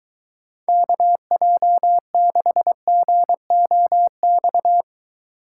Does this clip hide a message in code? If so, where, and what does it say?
Yes: Morse code from 0.7 to 4.8 s, heard as a keyed tone, about 22 wpm, 700 Hz. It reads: KJ6GOX